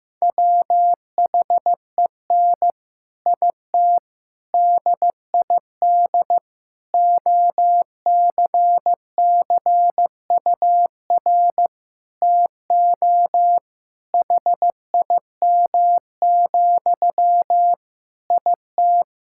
Morse code, heard as a keyed tone, about 15 wpm, 700 Hz. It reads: WHEN IT DID OCCUR TO HIM, IT
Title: Sentences from Top 1000 Words - Review Part 6 - 15wpm